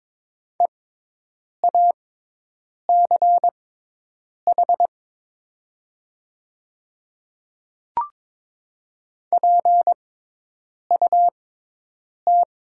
Morse code, heard as a keyed tone, about 22 wpm, spaced as for 8 wpm, 700 Hz. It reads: EACH PUT
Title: 6x Character Spacing - Top 200 Words - 22wpm